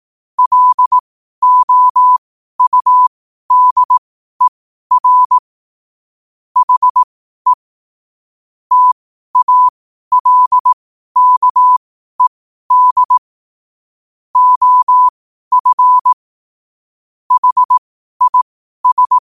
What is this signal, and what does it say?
Morse code, heard as a keyed tone, about 18 wpm, spaced as for 12 wpm, 1000 Hz. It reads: LOUDER HE TALKED OF HIS
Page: New quotes every day in morse code at 12 Words per minute.